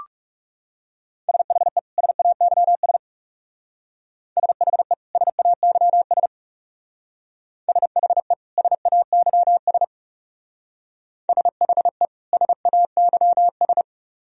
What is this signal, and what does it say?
Morse code, heard as a keyed tone, about 37 wpm, 700 Hz. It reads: SHE SAYS SHE SAYS SHE SAYS SHE SAYS
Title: Speed-Racing - Sets of 2 Words 30wpm